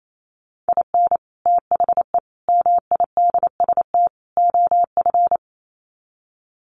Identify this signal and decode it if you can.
Morse code, heard as a keyed tone, about 28 wpm, 700 Hz. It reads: IN THE MIDST OF
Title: Sets of 4 Words 28wpm